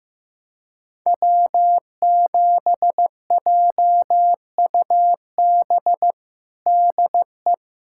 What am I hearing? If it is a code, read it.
Morse code, heard as a keyed tone, about 15 wpm, 700 Hz. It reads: W7JUB DE
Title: USA Callsign DE Callsign 15wpm